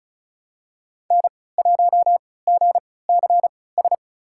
Morse code, heard as a keyed tone, about 36 wpm, spaced as for 20 wpm, 700 Hz. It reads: N1GCS
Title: USA Call Signs - 5 Characters - 3x Character Spacing - 35wpm